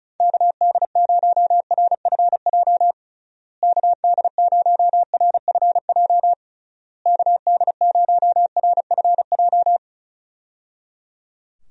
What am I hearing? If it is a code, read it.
Morse code, heard as a keyed tone, about 34 wpm, 700 Hz. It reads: KD0RFJ KD0RFJ KD0RFJ
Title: USA Call Signs - 6 Characters - Repeated 3x - 35wpm